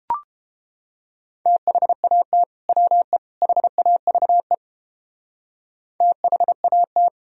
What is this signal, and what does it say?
Morse code, heard as a keyed tone, about 31 wpm, 700 Hz. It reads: THAT WE HAVE THAT